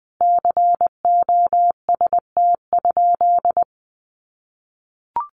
Morse code, heard as a keyed tone, about 20 wpm, 700 Hz. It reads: COST?
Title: Sentences from Top 1400 Words 20wpm